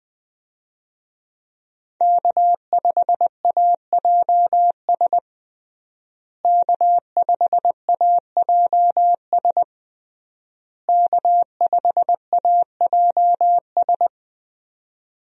Morse code, heard as a keyed tone, about 20 wpm, 700 Hz. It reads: K5AJS K5AJS K5AJS
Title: USA Call Signs - 5 Characters - Repeated 3x - 20wpm